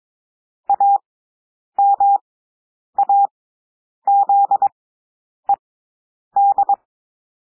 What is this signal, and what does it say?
Morse code, heard as a keyed tone, about 22 wpm, spaced as for 9 wpm, 800 Hz. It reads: AMAZED
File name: cwmp3 / W2 SOUND WORDS 22 wpm